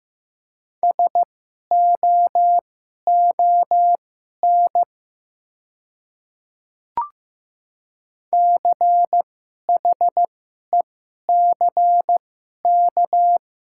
Morse code, heard as a keyed tone, about 15 wpm, spaced as for 11 wpm, 700 Hz. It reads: SOON CHECK
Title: 2x Character Spacing - Top 500 Words - 15wpm